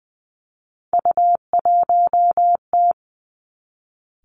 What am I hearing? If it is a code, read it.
Morse code, heard as a keyed tone, about 20 wpm, 700 Hz. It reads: U1T
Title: Three Challenging Letter-Number Combinations 20wpm